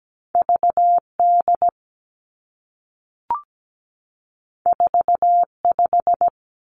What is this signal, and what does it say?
Morse code, heard as a keyed tone, about 17 wpm, 700 Hz. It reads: VD 45
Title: Two Challenging Letter-Number Combinations 17wpm